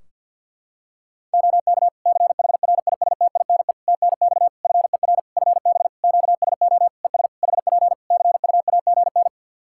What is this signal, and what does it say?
Morse code, heard as a keyed tone, about 49 wpm, 700 Hz. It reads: OK CHRISTINE TNX FER FB QSO ES HP CUAGN
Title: Practice QSO - Exchange 4 - Ending - 50wpm